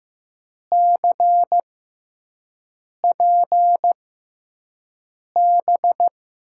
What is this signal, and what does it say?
Morse code, heard as a keyed tone, about 15 wpm, spaced as for 5 wpm, 700 Hz. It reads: CPB
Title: USA Call Signs - 6 Characters - 6x Character Spacing - 15wpm